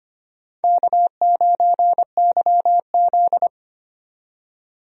Morse code, heard as a keyed tone, about 25 wpm, 700 Hz. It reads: K9YZ
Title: USA Call Signs - 4 Characters - 25wpm